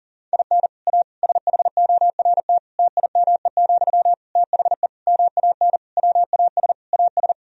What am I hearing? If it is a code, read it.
Morse code, heard as a keyed tone, about 40 wpm, 700 Hz. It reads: IN A SHORT TIME, THE MAN WAS AS
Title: Sentences from Top 1000 Words - Review Part 2 - 40wpm